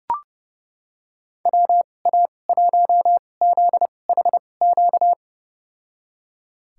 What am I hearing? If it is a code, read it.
Morse code, heard as a keyed tone, about 30 wpm, spaced as for 22 wpm, 700 Hz. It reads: WA1ZHQ
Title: USA Call Signs - 6 Characters - 2x Character Spacing - 30wpm